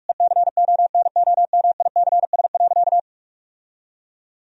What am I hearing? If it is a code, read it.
Morse code, heard as a keyed tone, about 46 wpm, 700 Hz. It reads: ECONOMICS.
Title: Sentences from Top 1000 Words - Review Part 3 - 45wpm